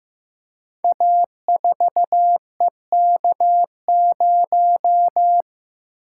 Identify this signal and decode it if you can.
Morse code, heard as a keyed tone, about 15 wpm, 700 Hz. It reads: A4EK0
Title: Code Groups - 5 Letter:Numbers - 15wpm